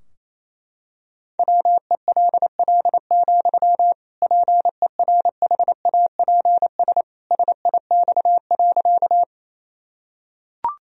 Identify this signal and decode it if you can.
Morse code, heard as a keyed tone, about 28 wpm, 700 Hz. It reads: WELL, PERHAPS SIX.
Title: Sentences from Top 1000 Words - Review Part 2 - 28wpm